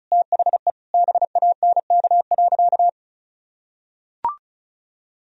Morse code, heard as a keyed tone, about 36 wpm, 700 Hz. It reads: THE BANK.
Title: Sentences from Top 1000 Words - Review Part 4 - 35wpm